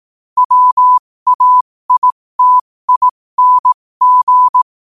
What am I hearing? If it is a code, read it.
Morse code, heard as a keyed tone, about 18 wpm, spaced as for 15 wpm, 1000 Hz. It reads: WAITING
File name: qotdfiles / 15wpm